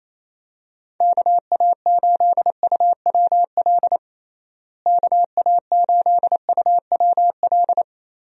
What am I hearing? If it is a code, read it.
Morse code, heard as a keyed tone, about 28 wpm, 700 Hz. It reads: KA8UWL KA8UWL
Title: USA Call Signs - 6 Characters - Repeated 3x - 28wpm